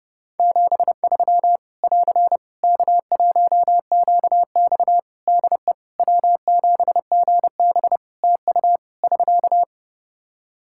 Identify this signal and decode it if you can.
Morse code, heard as a keyed tone, about 30 wpm, 700 Hz. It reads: 73 <AR> K1QX DE W7GB TU <SK>
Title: K1USN SST QSO Practice 30wpm